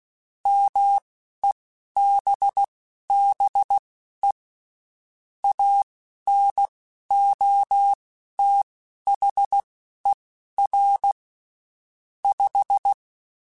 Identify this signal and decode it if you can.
Morse code, heard as a keyed tone, about 16 wpm, spaced as for 11 wpm, 800 Hz. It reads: MEBBE ANOTHER 5
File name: (spaced) / QSOs Spaced-mp3s / WA7QOX 1 @15 sp